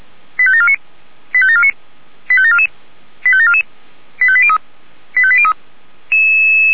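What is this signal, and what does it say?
Five-tone selective call (ZVEI), heard as quick runs of five tones, tones of about 70 ms. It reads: 85749 85749 85744 85744 85803 85803
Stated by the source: ZVEI five-tone selective call 85725 received on an SDR.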